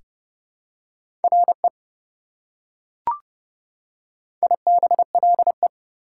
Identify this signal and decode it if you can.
Morse code, heard as a keyed tone, about 28 wpm, 700 Hz. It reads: RE IBLE
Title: Prefixes and Suffixes 30wpm